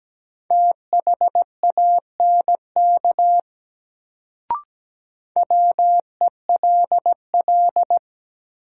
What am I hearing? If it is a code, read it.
Morse code, heard as a keyed tone, about 17 wpm, 700 Hz. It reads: THANK WELL
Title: Sets of 2 Words - New Word Drill - 17wpm